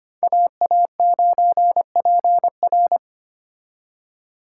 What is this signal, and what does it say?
Morse code, heard as a keyed tone, about 33 wpm, 700 Hz. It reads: AA9PR